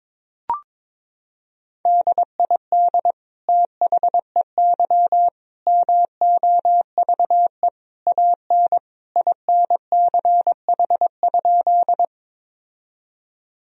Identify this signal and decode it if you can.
Morse code, heard as a keyed tone, about 21 wpm, 700 Hz. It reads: DID THEY MOVE AN INCH?
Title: Sentences from Top 1000 Words - Review Part 3 - 22wpm